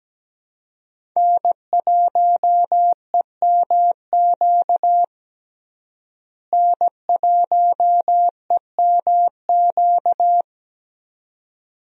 Morse code, heard as a keyed tone, about 17 wpm, 700 Hz. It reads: N1EMQ N1EMQ
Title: USA Call Signs - 5 Characters - Repeated 2x - 17wpm